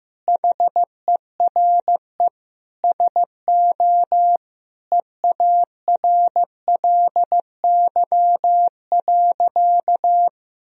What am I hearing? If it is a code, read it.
Morse code, heard as a keyed tone, about 15 wpm, 700 Hz. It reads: HERE SO EARLY.
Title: Sentences from Top 1000 Words - Review Part 2 - 15wpm